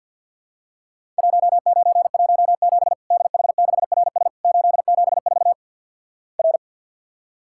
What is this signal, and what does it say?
Morse code, heard as a keyed tone, about 51 wpm, 700 Hz. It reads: 1917 DH6RS 874 R
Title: Worked All Europe DX Contest - QTC Exchange - 50wpm